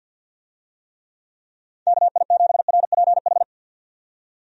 Morse code, heard as a keyed tone, about 51 wpm, 700 Hz. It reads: KI7RPH
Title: USA Call Signs - 6 Characters - 50wpm